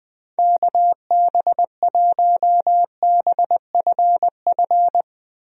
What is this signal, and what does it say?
Morse code, heard as a keyed tone, about 21 wpm, 700 Hz. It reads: KB1BFF
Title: USA Call Signs - 6 Characters - Repeated 3x - 20wpm